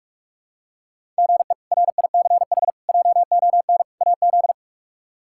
Morse code, heard as a keyed tone, about 46 wpm, 700 Hz. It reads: GE RICH JON AZ
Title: K1USN SST QSO Practice - 45wpm